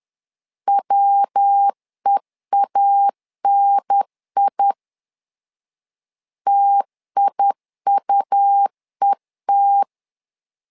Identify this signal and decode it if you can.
Morse code, heard as a keyed tone, about 10 wpm, 800 Hz. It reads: WEANI TIUET
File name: LEZIONI / 3 MWRO / 3 MW 10wpm 1959